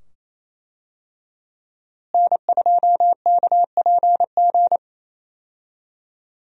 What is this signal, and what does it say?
Morse code, heard as a keyed tone, about 28 wpm, 700 Hz. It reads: N2KPG
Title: USA Call Signs - 5 Characters - Repeated 2x - 28wpm